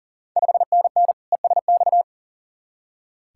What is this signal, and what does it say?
Morse code, heard as a keyed tone, about 39 wpm, 690 Hz. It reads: 5NN ESX
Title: Ontario QSO Party Exchange 40wpm